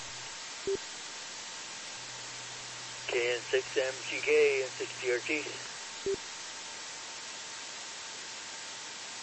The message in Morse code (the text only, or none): none